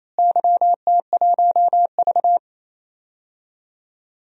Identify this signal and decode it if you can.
Morse code, heard as a keyed tone, about 28 wpm, 700 Hz. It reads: YT1V
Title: DXCC-100 Call Signs - 4 Characters - Repeated 2x - 28wpm